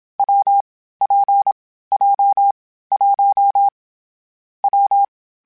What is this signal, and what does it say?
Morse code, heard as a keyed tone, about 27 wpm, 800 Hz. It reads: W P J 1 W
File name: Difficult Combos / Difficult Combos-mp3s / WPJ WPJ1 @25